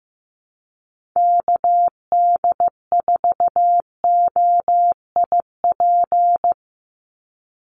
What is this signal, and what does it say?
Morse code, heard as a keyed tone, about 15 wpm, 700 Hz. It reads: KD4OIP
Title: USA Call Signs 15wpm